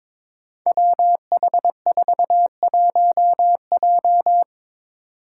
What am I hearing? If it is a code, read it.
Morse code, heard as a keyed tone, about 22 wpm, 700 Hz. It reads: WH41J
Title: Five Challenging Letter-Number Combinations 22wpm